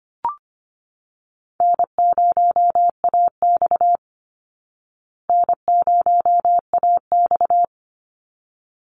Morse code, heard as a keyed tone, about 25 wpm, 700 Hz. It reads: N0AX N0AX
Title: Easy Call Signs Repeated 2x - 25wpm